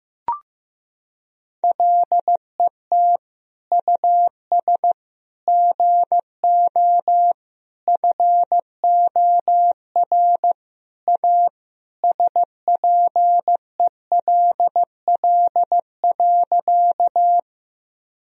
Morse code, heard as a keyed tone, about 16 wpm, 700 Hz. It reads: LET US GO FOR A SPELL.
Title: Sentences from Top 1000 Words - Review Part 1 - 15wpm